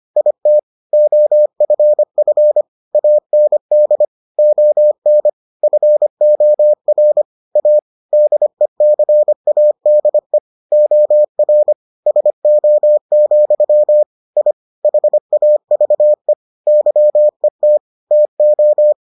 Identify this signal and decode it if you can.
Morse code, heard as a keyed tone, about 25 wpm, 600 Hz. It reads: IT OFF AND ON FOR A DECADE OR SO, I HAVE YET TO